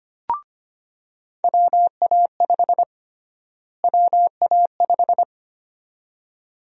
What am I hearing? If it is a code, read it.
Morse code, heard as a keyed tone, about 25 wpm, 700 Hz. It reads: WA5 WA5
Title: USA Call Sign Prefixes - Repeated 2x - 25wpm